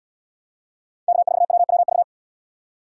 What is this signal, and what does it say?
Morse code, heard as a keyed tone, about 77 wpm, 700 Hz. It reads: K4WWV